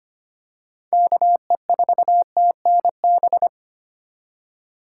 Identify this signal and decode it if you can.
Morse code, heard as a keyed tone, about 25 wpm, 700 Hz. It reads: KE4TNB
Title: USA Call Signs - 6 Characters - 4x Character Spacing - 25wpm